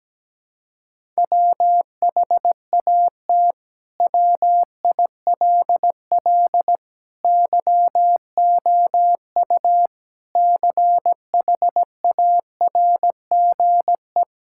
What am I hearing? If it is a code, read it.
Morse code, heard as a keyed tone, about 17 wpm, 700 Hz. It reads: WHAT WILL YOU CHARGE
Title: Sentences from Top 1000 Words - Review Part 6 - 17wpm